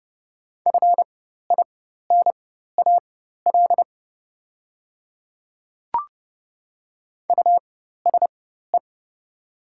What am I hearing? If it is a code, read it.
Morse code, heard as a keyed tone, about 30 wpm, spaced as for 14 wpm, 700 Hz. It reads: FINAL USE
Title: 4x Character Spacing - Top 500 Words - 30wpm